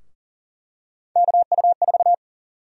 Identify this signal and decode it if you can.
Morse code, heard as a keyed tone, about 40 wpm, 700 Hz. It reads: KU4
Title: USA Call Sign Prefixes - Repeated 2x - 40wpm